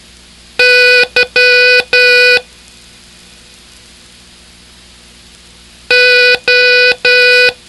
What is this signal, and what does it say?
Morse code, heard as a keyed tone, about 11 wpm, 1480 Hz. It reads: Y O